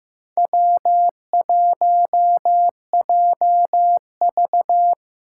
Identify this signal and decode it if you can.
Morse code, heard as a keyed tone, about 15 wpm, 700 Hz. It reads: W1JV
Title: USA Call Signs - 4 Characters - 2x Character Spacing - 15wpm